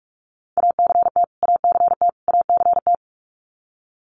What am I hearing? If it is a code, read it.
Morse code, heard as a keyed tone, about 44 wpm, 700 Hz. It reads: ACT ACT ACT
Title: Top 500 Words Repeated 45wpm